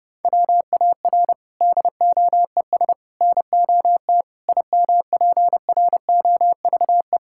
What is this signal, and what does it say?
Morse code, heard as a keyed tone, about 30 wpm, 700 Hz. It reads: WAR DOES NOT IMPROVE
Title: Sentences from Top 1000 Words - Review Part 5 - 30wpm